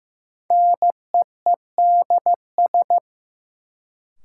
Morse code, heard as a keyed tone, about 16 wpm, 700 Hz. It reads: NEEDS
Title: Sentences from Top 1000 Words - Review Part 3 - New Word Drill - 15wpm